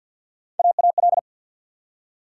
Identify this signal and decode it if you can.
Morse code, heard as a keyed tone, about 48 wpm, 700 Hz. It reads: AAL
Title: USA Call Sign Postfixes - 3 Letters 50wpm